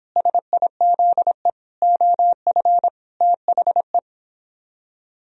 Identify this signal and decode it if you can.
Morse code, heard as a keyed tone, about 26 wpm, 700 Hz. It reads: SIZE OF THE